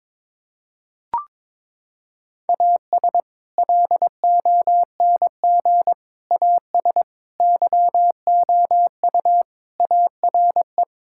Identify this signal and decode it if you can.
Morse code, heard as a keyed tone, about 22 wpm, 700 Hz. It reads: AS LONG AS YOU ARE
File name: practice / 22wpm / Sentences with 4-Word Phrases - 22wpm